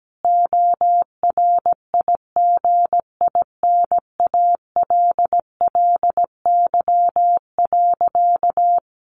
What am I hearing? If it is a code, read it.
Morse code, heard as a keyed tone, about 17 wpm, 700 Hz. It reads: ORIGINALLY.
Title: Sentences from Top 1500 Words 17wpm